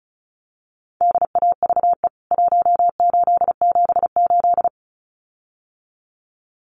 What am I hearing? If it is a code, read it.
Morse code, heard as a keyed tone, about 34 wpm, 700 Hz. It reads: DAVE 1878
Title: CWT Exchange 35wpm